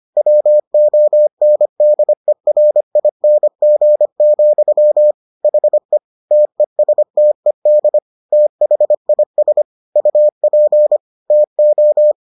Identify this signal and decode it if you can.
Morse code, heard as a keyed tone, about 25 wpm, 600 Hz. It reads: WONDERING, HE TESTED THIS UP TO